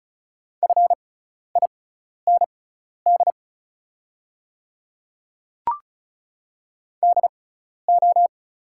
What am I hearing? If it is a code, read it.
Morse code, heard as a keyed tone, about 36 wpm, spaced as for 12 wpm, 690 Hz. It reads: FIND DO
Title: 6x Character Spacing - Top 300 Words - 35wpm